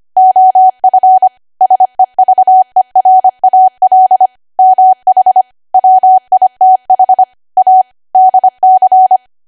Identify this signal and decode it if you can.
Morse code, heard as a keyed tone, about 25 wpm, 750 Hz. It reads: OF SEVERAL MH WITH A DC
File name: cwa / W1AW-Level3 / 25WPM